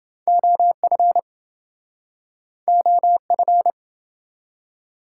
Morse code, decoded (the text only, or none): OF OF